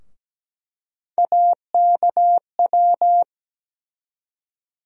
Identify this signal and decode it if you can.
Morse code, heard as a keyed tone, about 17 wpm, 700 Hz. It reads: AKW
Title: USA Call Sign Postfixes - 3 Letters 17wpm